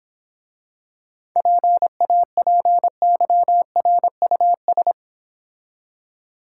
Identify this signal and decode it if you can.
Morse code, heard as a keyed tone, about 26 wpm, 700 Hz. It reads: PAPYRUS